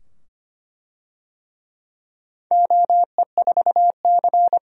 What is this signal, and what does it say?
Morse code, heard as a keyed tone, about 25 wpm, 700 Hz. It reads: OE4C
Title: DXCC-100 Call Signs - 4 Characters - 25wpm